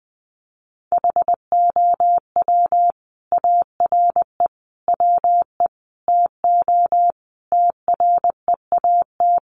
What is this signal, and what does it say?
Morse code, heard as a keyed tone, about 19 wpm, 700 Hz. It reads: HOW ARE WE TO TREAT
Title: Sentences from Top 1300 Words 20wpm